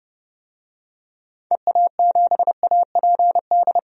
Morse code, heard as a keyed tone, about 32 wpm, 700 Hz. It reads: EA7APD
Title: DXCC-100 Call Signs - 6 Characters - 30wpm